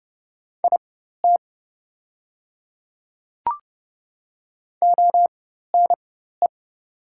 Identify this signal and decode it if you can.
Morse code, heard as a keyed tone, about 30 wpm, spaced as for 14 wpm, 700 Hz. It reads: IT ONE
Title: 4x Character Spacing - Top 200 Words - 30wpm